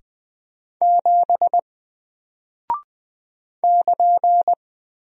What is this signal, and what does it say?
Morse code, heard as a keyed tone, about 20 wpm, 700 Hz. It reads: 7 <KN>
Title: Single Character (Letter:Number:Punctuation:Prosign) Alt 20wpm